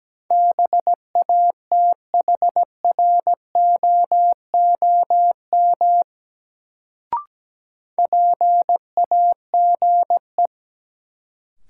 Morse code, heard as a keyed tone, about 17 wpm, 700 Hz. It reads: BATHROOM PAGE
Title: Sets of 5 Words Encore - New Word Drill - 17wpm